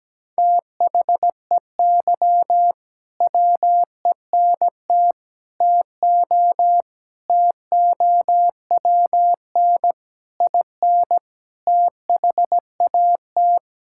Morse code, heard as a keyed tone, about 17 wpm, 700 Hz. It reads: THEY WENT TO TOWN IN THAT
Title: Sentences from Top 1000 Words - Review Part 5 - 17wpm